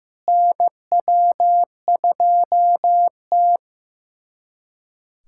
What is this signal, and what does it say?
Morse code, heard as a keyed tone, about 15 wpm, 700 Hz. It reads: NW2T
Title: USA Callsign DE Callsign - 4 Characters - 15wpm